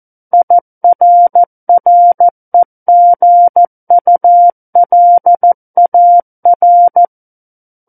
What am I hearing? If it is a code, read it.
Morse code, heard as a keyed tone, about 14 wpm, 700 Hz. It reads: IRREGULAR
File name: Book One / 05-Chapter 5 - 14 WPM